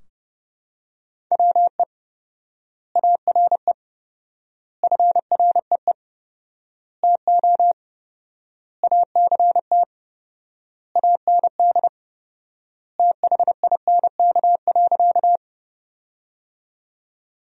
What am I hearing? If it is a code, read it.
Morse code, heard as a keyed tone, about 30 wpm, 700 Hz. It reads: WE ARE FREE TO ACT AND THINK.
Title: 4x Word-Spacing - Sentences from Top 500 Words - 30wpm